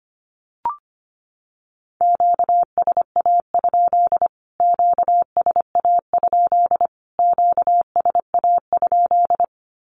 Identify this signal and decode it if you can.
Morse code, heard as a keyed tone, about 25 wpm, 700 Hz. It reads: QSA? QSA? QSA?
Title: Q-Codes Repeated 3x 25wpm